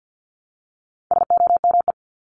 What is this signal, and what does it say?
Morse code, heard as a keyed tone, about 50 wpm, 700 Hz. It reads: SOME
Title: Top 500 Words 50wpm